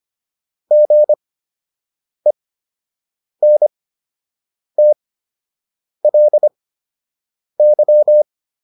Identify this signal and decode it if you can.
Morse code, heard as a keyed tone, about 25 wpm, spaced as for 7 wpm, 600 Hz. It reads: GENTLY